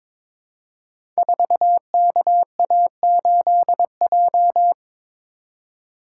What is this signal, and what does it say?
Morse code, heard as a keyed tone, about 22 wpm, 700 Hz. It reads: 4KA8J
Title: Code Groups - 5 Letter:Numbers - 22wpm